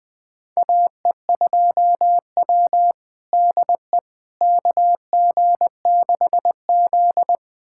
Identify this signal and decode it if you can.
Morse code, heard as a keyed tone, about 21 wpm, 700 Hz. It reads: AE2W DE KG6Z
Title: USA Callsign DE Callsign - 4 Characters - 20wpm